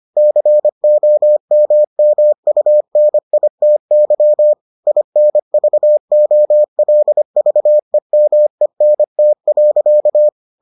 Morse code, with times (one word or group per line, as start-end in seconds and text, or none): 0.2-4.5 COMMUNITY
4.9-10.3 INVOLVEMENT.